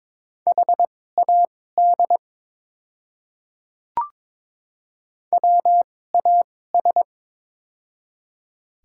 Morse code, heard as a keyed tone, about 22 wpm, spaced as for 16 wpm, 700 Hz. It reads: HAD WAS
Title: 2x Character Spacing - Top 200 Words - 22wpm